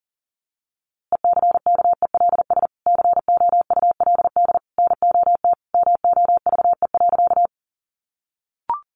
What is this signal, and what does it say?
Morse code, heard as a keyed tone, about 40 wpm, 700 Hz. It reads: ECKELS COULD NOT MOVE.
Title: A Sound Of Thunder 40wpm